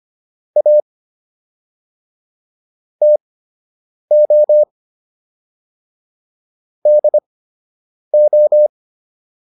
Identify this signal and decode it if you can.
Morse code, heard as a keyed tone, about 25 wpm, spaced as for 8 wpm, 600 Hz. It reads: A TO DO